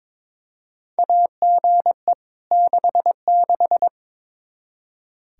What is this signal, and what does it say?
Morse code, heard as a keyed tone, about 22 wpm, 700 Hz. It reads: AGE 66
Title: Practice QSO - Exchange 3 - More Info - 22wpm